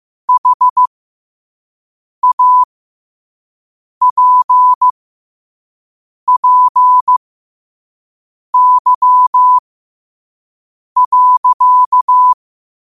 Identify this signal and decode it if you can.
Morse code, heard as a keyed tone, about 15 wpm, spaced as for 5 wpm, 1000 Hz. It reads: HAPPY.
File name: qotdfiles / 5wpm